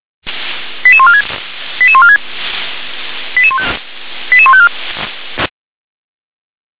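Five-tone selective call (ZVEI), heard as quick runs of five tones, tones of about 70 ms. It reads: 88146 88146 88154
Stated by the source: ZVEI five-tone selective call received by SDR.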